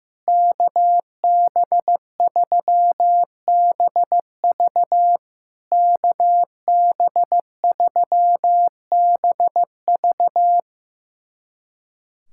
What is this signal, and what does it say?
Morse code, heard as a keyed tone, about 15 wpm, 700 Hz. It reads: KB3BV KB3BV
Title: SOTA QSO Practice 15wpm